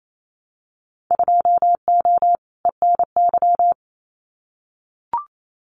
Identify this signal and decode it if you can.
Morse code, heard as a keyed tone, about 28 wpm, 700 Hz. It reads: 2O ENY
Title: Winter Field Day Exchanges Encore 28wpm